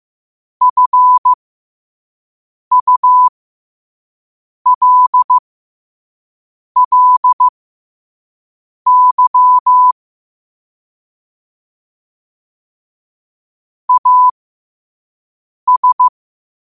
Morse code, heard as a keyed tone, about 15 wpm, spaced as for 5 wpm, 1000 Hz. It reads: FULLY AS
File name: qotdfiles / 5wpm